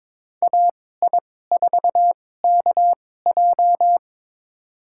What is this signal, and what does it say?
Morse code, heard as a keyed tone, about 22 wpm, spaced as for 16 wpm, 700 Hz. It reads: AI4KJ
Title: USA Call Signs - 2x Character Spacing - 22wpm